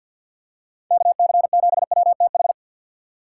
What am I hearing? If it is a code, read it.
Morse code, heard as a keyed tone, about 52 wpm, 690 Hz. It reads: KC7WTH
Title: USA Call Signs - 3x Character Spacing - 50wpm